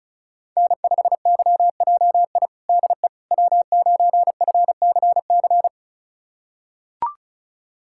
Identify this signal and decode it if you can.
Morse code, heard as a keyed tone, about 36 wpm, 700 Hz. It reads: N5YJI DE W9FCC
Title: USA Callsign DE Callsign - 5 Characters - 35wpm